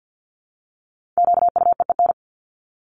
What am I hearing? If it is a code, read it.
Morse code, heard as a keyed tone, about 50 wpm, 700 Hz. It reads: QUEEN
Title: Top 1751-2000 Words 50wpm